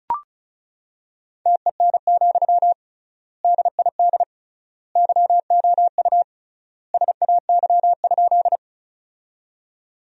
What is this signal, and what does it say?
Morse code, heard as a keyed tone, about 36 wpm, 700 Hz. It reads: TEN, DID YOU SAY?
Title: 3x Word-Spacing - Sentences from Top 400 Words - 35wpm